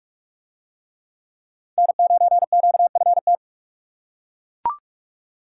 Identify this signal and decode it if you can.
Morse code, heard as a keyed tone, about 42 wpm, 700 Hz. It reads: N9QFT